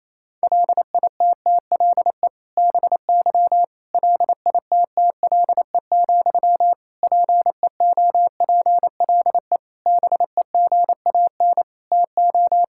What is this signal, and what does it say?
Morse code, heard as a keyed tone, about 28 wpm, 700 Hz. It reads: LITTLE BY LITTLE, PEOPLE BEGAN TO
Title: Sentences from Top 1600 Words 28wpm